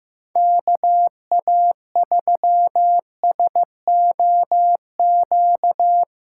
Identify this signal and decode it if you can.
Morse code, heard as a keyed tone, about 15 wpm, 700 Hz. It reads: KA3SOQ
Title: USA Callsign DE Callsign 15wpm